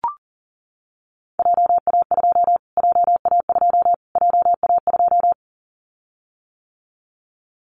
Morse code, heard as a keyed tone, about 40 wpm, 700 Hz. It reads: JA2 JA2 JA2
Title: Callsign Prefixes for Top 25 DXCC - Repeated 3x - 40wpm